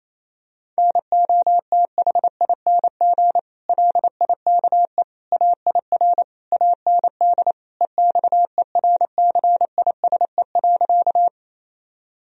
Morse code, heard as a keyed tone, about 28 wpm, 700 Hz. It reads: NOTHING LIKE AIR AND EXERCISE.
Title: Sentences from Top 1000 Words - Review Part 4 - 28wpm